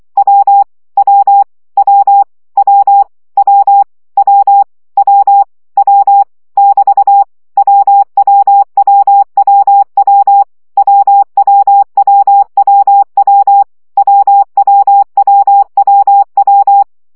Morse code, heard as a keyed tone, about 24 wpm, 800 Hz. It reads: W W W W W W W W <BT> WWWWW WWWWW WWWWW
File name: CW gyakorlas w 20 baud